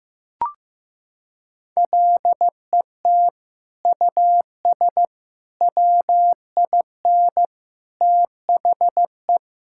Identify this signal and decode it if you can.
Morse code, heard as a keyed tone, about 15 wpm, 700 Hz. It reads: LET US WIN THE
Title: Sentences from Top 1600 Words 15wpm